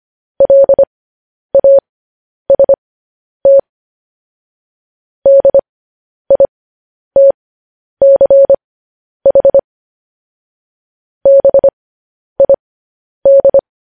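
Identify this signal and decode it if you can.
Morse code, heard as a keyed tone, about 25 wpm, spaced as for 10 wpm, 550 Hz. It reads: LAST DITCH BID
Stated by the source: Morse code news headlines sent at 10 wpm for practice.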